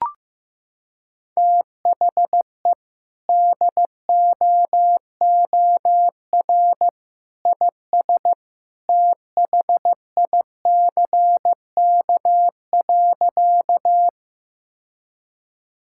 Morse code, heard as a keyed tone, about 16 wpm, 700 Hz. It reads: THE DOOR IS THICK.
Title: Sentences from Top 1600 Words 15wpm